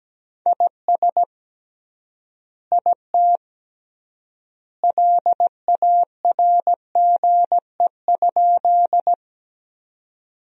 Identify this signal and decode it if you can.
Morse code, heard as a keyed tone, about 17 wpm, 700 Hz. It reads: IS IT LARGE?
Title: 3x Word-Spacing - Sentences from Top 200 Words - 17wpm